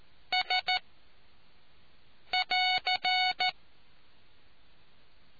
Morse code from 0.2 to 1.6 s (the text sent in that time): S